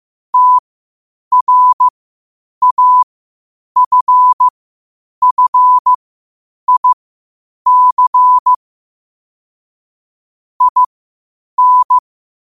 Morse code, heard as a keyed tone, about 15 wpm, spaced as for 8 wpm, 1000 Hz. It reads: TRAFFIC IN
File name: qotdfiles / 8wpm